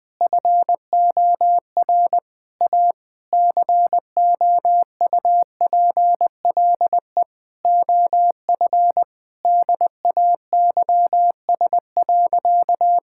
Morse code, heard as a keyed tone, about 20 wpm, 700 Hz. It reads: FOR A COUPLE OF DAYS.
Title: Sentences with 4-Word Phrases - 20wpm